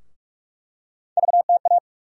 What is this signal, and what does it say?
Morse code, heard as a keyed tone, about 47 wpm, 700 Hz. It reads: VTA
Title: USA Call Sign Postfixes - 3 Letters 45wpm